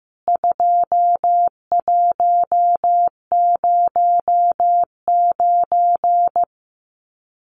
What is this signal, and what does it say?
Morse code, heard as a keyed tone, about 15 wpm, 700 Hz. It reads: 2109
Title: CWT Exchange 15wpm